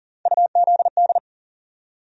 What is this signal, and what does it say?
Morse code, heard as a keyed tone, about 40 wpm, 700 Hz. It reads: UZD
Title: USA Call Sign Postfixes - 3 Letters 40wpm